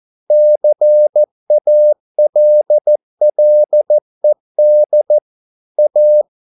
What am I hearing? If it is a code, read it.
Morse code, heard as a keyed tone, about 14 wpm, 600 Hz. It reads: CALLED A